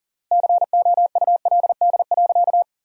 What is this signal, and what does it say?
Morse code, heard as a keyed tone, about 40 wpm, 700 Hz. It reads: COULD.
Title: 3x Word-Spacing - Sentences from Top 500 Words - 40wpm